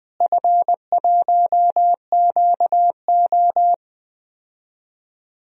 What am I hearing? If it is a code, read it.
Morse code, heard as a keyed tone, about 20 wpm, 700 Hz. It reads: F1QO
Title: DXCC-100 Call Signs - 4 Characters - Repeated 2x - 20wpm